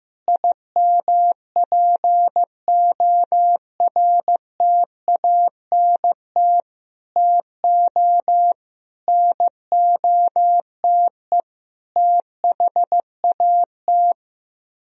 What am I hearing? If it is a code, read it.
Morse code, heard as a keyed tone, about 15 wpm, 700 Hz. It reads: IMPORTANT TO NOTE THAT
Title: Speed-Racing - Sets of 5 Words 15wpm